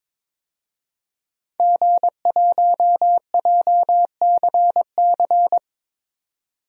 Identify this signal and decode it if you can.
Morse code, heard as a keyed tone, about 22 wpm, 700 Hz. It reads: G1JCC
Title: DXCC-100 Call Signs - 5 Characters - 22wpm